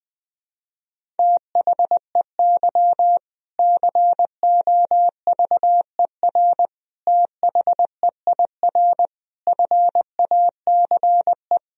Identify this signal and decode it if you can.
Morse code, heard as a keyed tone, about 20 wpm, 700 Hz. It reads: THEY COVER THEIR FACE
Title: Sentences from Top 1000 Words - Review Part 1 - 20wpm